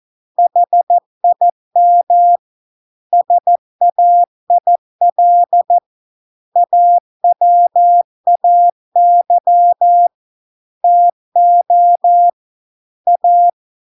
Morse code, heard as a keyed tone, about 14 wpm, 700 Hz. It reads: HIM SAIL AWAY TO A